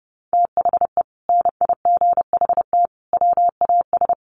Morse code, heard as a keyed tone, about 30 wpm, 700 Hz. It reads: THE NIGHT WAS
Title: Sentences from Top 1200 Words 30wpm